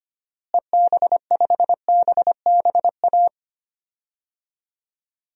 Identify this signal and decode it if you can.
Morse code, heard as a keyed tone, about 25 wpm, 700 Hz. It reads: EB5BBA
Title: DXCC-100 Call Signs - 6 Characters - Repeated 3x - 25wpm